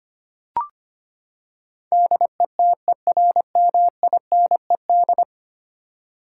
Morse code, heard as a keyed tone, about 25 wpm, 700 Hz. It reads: DETERMINED
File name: Ten Letter Words - 25wpm